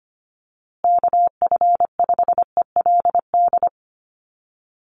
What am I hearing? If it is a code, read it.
Morse code, heard as a keyed tone, about 25 wpm, 700 Hz. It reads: KF5ELD
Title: USA Call Signs Repeated 2x 25wpm